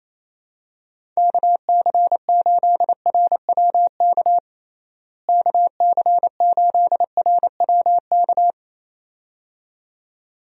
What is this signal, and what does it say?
Morse code, heard as a keyed tone, about 28 wpm, 700 Hz. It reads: KC8RWK KC8RWK
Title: USA Call Signs - 6 Characters - Repeated 2x - 28wpm